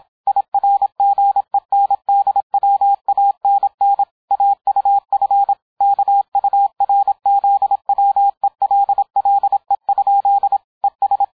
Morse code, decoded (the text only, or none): IRGENDWANN AUF KURZWELLE? ES